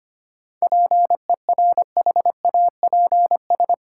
Morse code, heard as a keyed tone, about 25 wpm, 700 Hz. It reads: PERHAPS
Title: Sentences from Top 500 Words - New Word Drill - 25wpm